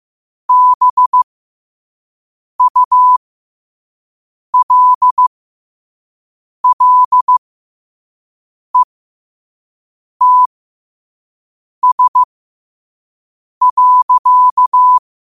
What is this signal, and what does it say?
Morse code, heard as a keyed tone, about 15 wpm, spaced as for 5 wpm, 1000 Hz. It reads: BULLETS.
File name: qotdfiles / 5wpm